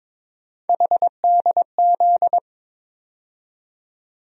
Morse code, heard as a keyed tone, about 22 wpm, 700 Hz. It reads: HDZ
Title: USA Call Sign Postfixes - 3 Letters 22wpm